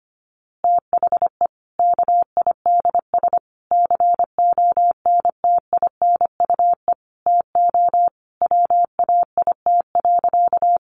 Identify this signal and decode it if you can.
Morse code, heard as a keyed tone, about 25 wpm, 700 Hz. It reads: THE KIDS CONTINUE TO WAIT.
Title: Easy Phrases 25wpm